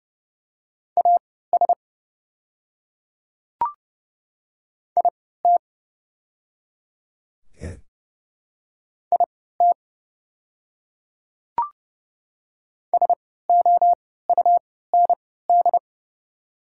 Morse code, heard as a keyed tone, about 29 wpm, spaced as for 17 wpm, 690 Hz. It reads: AS IT IT SOUND
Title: 3x Character Spacing - Top 200 Words - 30wpm